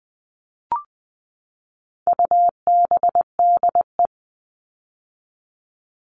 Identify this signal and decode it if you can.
Morse code, heard as a keyed tone, about 20 wpm, 700 Hz. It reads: UBDE
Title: All Cut Numbers - 4-Digit 20wpm